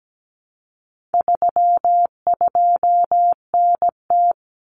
Morse code, heard as a keyed tone, about 17 wpm, 700 Hz. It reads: 32NT
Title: Common Cut Numbers - 4-Digit 17wpm